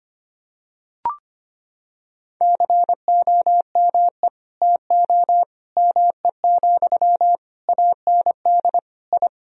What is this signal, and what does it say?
Morse code, heard as a keyed tone, about 25 wpm, 700 Hz. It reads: COME TO ME, AND I
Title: Sentences from Top 1000 Words - Review Part 2 - 25wpm